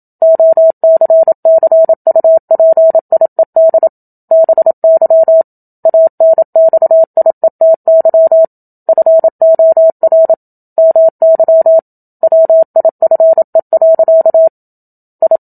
Morse code, heard as a keyed tone, about 27 wpm, 650 Hz. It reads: OCCUPIED BY ANXIETY FOR MY WIFE. I
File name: Book Two / 01-Chapter 1 - 27 WPM